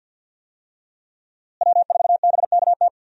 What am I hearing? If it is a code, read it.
Morse code, heard as a keyed tone, about 50 wpm, 700 Hz. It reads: W4BKT
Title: USA Call Signs - 5 Characters - Repeated 2x - 50wpm